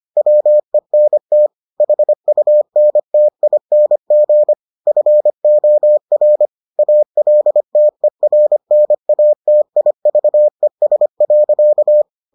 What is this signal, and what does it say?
Morse code, heard as a keyed tone, about 25 wpm, 600 Hz. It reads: WENT HUNTING FOR ALTERNATIVES.